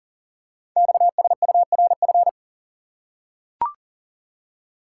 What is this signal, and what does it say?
Morse code, heard as a keyed tone, about 40 wpm, 700 Hz. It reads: XSURF
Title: Code Groups - 5 Letters - 40wpm